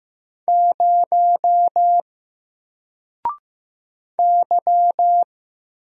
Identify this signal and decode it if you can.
Morse code, heard as a keyed tone, about 15 wpm, 700 Hz. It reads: 0 Y
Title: Single Character (Letter:Number:Punctuation:Prosign) 15wpm